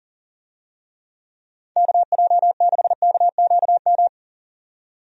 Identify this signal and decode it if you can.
Morse code, heard as a keyed tone, about 37 wpm, 700 Hz. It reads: KJ6KQM
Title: USA Call Signs - 6 Characters - 40wpm